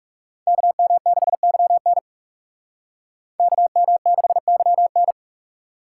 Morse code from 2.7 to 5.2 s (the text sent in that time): KM6YN